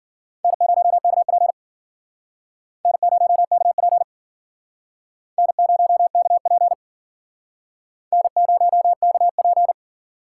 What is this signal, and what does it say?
Morse code, heard as a keyed tone, about 50 wpm, 700 Hz. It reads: N0KP N0KP N0KP N0KP